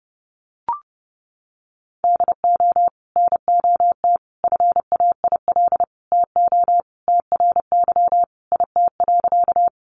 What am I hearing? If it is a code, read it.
Morse code, heard as a keyed tone, about 30 wpm, 700 Hz. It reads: DO NOT FAIL TO TRY IT.
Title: Sentences from Top 1100 Words 30wpm